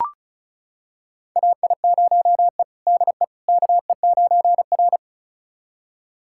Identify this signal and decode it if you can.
Morse code, heard as a keyed tone, about 34 wpm, 700 Hz. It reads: AI0E DE KE9R
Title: USA Callsign DE Callsign - 4 Characters - 35wpm